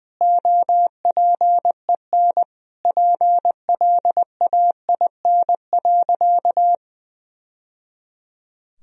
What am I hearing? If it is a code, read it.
Morse code, heard as a keyed tone, about 20 wpm, 700 Hz. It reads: OPEN PLAIN.
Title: Sentences from Top 1000 Words - Review Part 2 - 20wpm